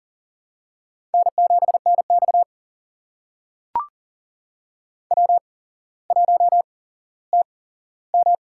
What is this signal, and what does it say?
Morse code, heard as a keyed tone, about 40 wpm, spaced as for 17 wpm, 700 Hz. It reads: N7NX W1TM
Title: USA Call Signs - 4 Characters - 8x Character Spacing - 40wpm